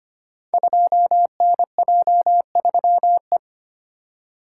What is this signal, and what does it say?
Morse code, heard as a keyed tone, about 25 wpm, 700 Hz. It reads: 2NJ3E
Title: Code Groups - 5 Letter:Numbers - 25wpm